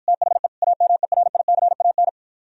Morse code, heard as a keyed tone, about 52 wpm, 700 Hz. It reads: THE AMERICAN